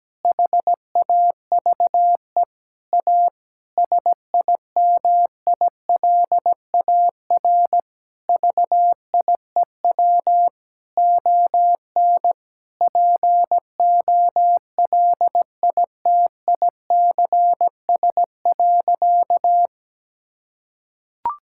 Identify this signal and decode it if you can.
Morse code, heard as a keyed tone, about 17 wpm, 700 Hz. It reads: HAVE A SIMILAR VIEW ON POLITICS.
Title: Sentences from Top 1000 Words - Review Part 6 - 17wpm